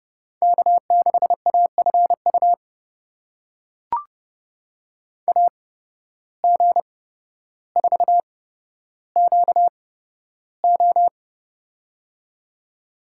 Morse code, heard as a keyed tone, about 30 wpm, spaced as for 13 wpm, 700 Hz. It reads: K6AFU AG4QO
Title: USA Call Signs - 5 Characters - 8x Character Spacing - 30wpm